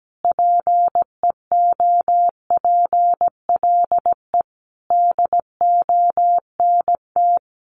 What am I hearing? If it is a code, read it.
Morse code, heard as a keyed tone, about 17 wpm, 700 Hz. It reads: PEOPLE DONT
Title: Sentences from Top 600 Words 17wpm